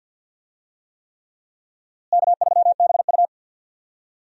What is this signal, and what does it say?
Morse code, heard as a keyed tone, about 46 wpm, 700 Hz. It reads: K3BU
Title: USA Call Signs - 4 Characters - Repeated 3x - 50wpm